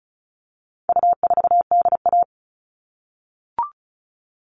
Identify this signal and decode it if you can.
Morse code, heard as a keyed tone, about 34 wpm, 700 Hz. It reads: U4DA
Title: All Cut Numbers - 4-Digit 35wpm